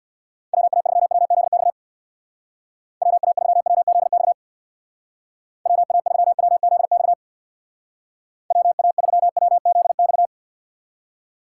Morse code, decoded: WA3WZX WA3WZX WA3WZX WA3WZX

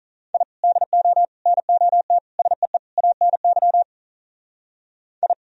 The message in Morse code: I DO NOT SEE ANY I

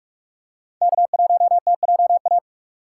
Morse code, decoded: K1TJA

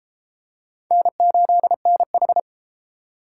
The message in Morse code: N8NH

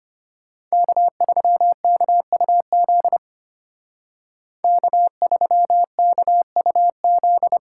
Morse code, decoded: K3KUZ K3KUZ

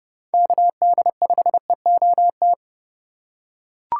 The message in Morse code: KD5EOT

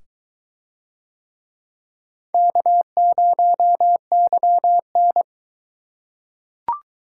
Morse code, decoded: K0YN

